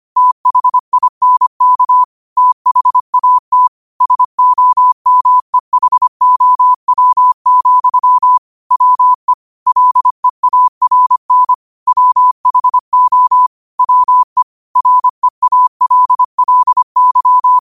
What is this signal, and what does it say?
Morse code, heard as a keyed tone, about 25 wpm, 1000 Hz. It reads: THINK THAT SOMEHOW, WE LEARN WHO WE REALLY
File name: qotdfiles / 25wpm